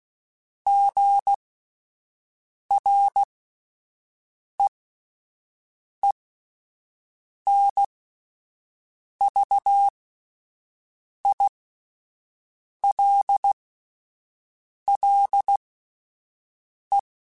Morse code, heard as a keyed tone, about 16 wpm, spaced as for 5 wpm, 800 Hz. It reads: GREENVILLE